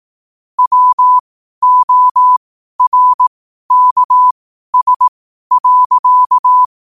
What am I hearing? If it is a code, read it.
Morse code, heard as a keyed tone, about 18 wpm, spaced as for 12 wpm, 1000 Hz. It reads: WORKS.
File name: qotdfiles / 12wpm